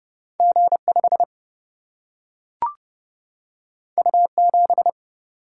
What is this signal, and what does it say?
Morse code, heard as a keyed tone, about 30 wpm, 700 Hz. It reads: G5 U7
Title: Single-Letter Single-Number 30wpm